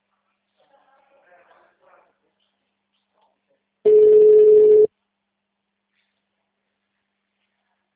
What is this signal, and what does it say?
Ringback tone (Europe, 425 Hz), with speech.
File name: out- -2017- - - .1747